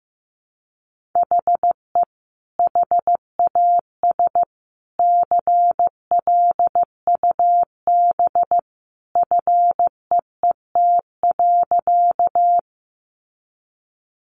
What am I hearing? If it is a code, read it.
Morse code, heard as a keyed tone, about 15 wpm, 700 Hz. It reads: HE HAS CLUB FEET.
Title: Sentences from Top 1200 Words 15wpm